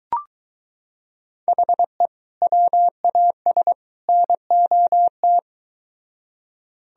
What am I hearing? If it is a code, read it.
Morse code, heard as a keyed tone, about 23 wpm, 700 Hz. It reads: HE WAS NOT